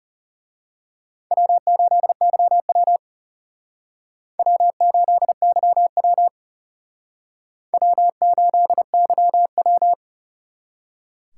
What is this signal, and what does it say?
Morse code, heard as a keyed tone, about 36 wpm, 700 Hz. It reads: W8YW W8YW W8YW
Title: Speed Racing - USA 1x2 Call Signs 30wpm